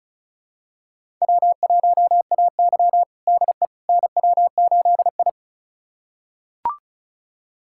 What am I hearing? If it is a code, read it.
Morse code, heard as a keyed tone, about 36 wpm, 700 Hz. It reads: W1AY DE NW8I
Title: USA Callsign DE Callsign - 4 Characters - 35wpm